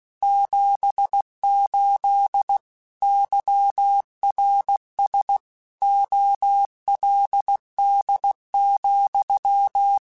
Morse code, heard as a keyed tone, about 16 wpm, 800 Hz. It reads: 78 YRS OLD,
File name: QSOs / QSOs-mp3s / WA7QOX 2@15